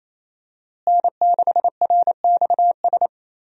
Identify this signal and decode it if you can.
Morse code, heard as a keyed tone, about 28 wpm, 700 Hz. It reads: N6RXS
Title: USA Call Signs - 5 Characters - Repeated 3x - 28wpm